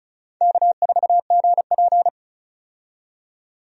Morse code, heard as a keyed tone, about 34 wpm, 700 Hz. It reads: K4GP
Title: USA Call Signs - 4 Characters - 35wpm